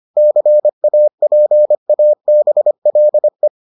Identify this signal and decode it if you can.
Morse code, heard as a keyed tone, about 25 wpm, 600 Hz. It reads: CAPABLE